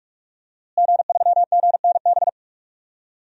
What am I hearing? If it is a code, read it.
Morse code, heard as a keyed tone, about 46 wpm, 700 Hz. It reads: G3GNB
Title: DXCC-100 Call Signs - 5 Characters - Repeated 3x - 45wpm